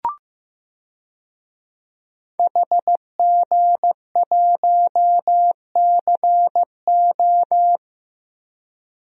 Morse code, heard as a keyed tone, about 15 wpm, 700 Hz. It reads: HG1CO
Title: DXCC-100 Call Signs - 5 Characters - 15wpm